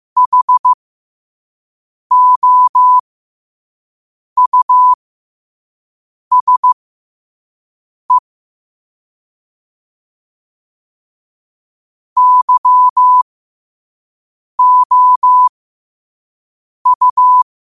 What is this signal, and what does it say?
Morse code, heard as a keyed tone, about 15 wpm, spaced as for 5 wpm, 1000 Hz. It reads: HOUSE YOU
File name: qotdfiles / 5wpm